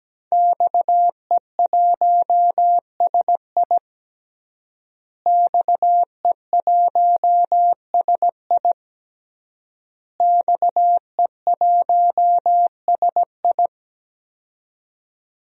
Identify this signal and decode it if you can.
Morse code, heard as a keyed tone, about 17 wpm, 700 Hz. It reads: XE1SI XE1SI XE1SI
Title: DXCC-100 Call Signs - 5 Characters - Repeated 3x - 17wpm